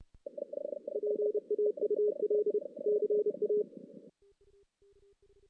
Morse code, heard as a keyed tone, about 28 wpm, 410 Hz. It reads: PAUL CA
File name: 2016 JAN CW NAQP / 2016 Jan naqp k5ot - 0-5